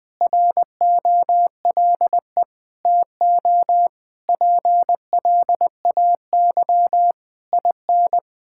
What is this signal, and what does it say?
Morse code, heard as a keyed tone, about 20 wpm, 700 Hz. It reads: ROLE TO PLAY IN
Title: Sentences from Top 1000 Words - Review Part 6 - 20wpm